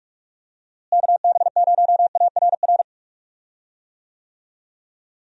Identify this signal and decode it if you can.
Morse code, heard as a keyed tone, about 44 wpm, 700 Hz. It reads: KB0ARR
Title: USA Call Signs - 6 Characters - Repeated 3x - 45wpm